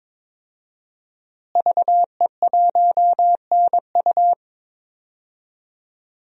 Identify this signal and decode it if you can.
Morse code, heard as a keyed tone, about 22 wpm, 700 Hz. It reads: VE1NU
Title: DXCC-100 Call Signs - 5 Characters - Repeated 2x - 22wpm